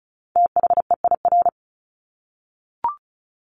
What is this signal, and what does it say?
Morse code, heard as a keyed tone, about 34 wpm, 700 Hz. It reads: THEIR
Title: Top 200 Words Repeated 35wpm